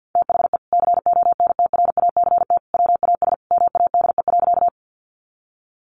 Morse code, heard as a keyed tone, about 50 wpm, 700 Hz. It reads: THE CONTRACT WAS MADE.
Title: Sentences from Top 1400 Words 50wpm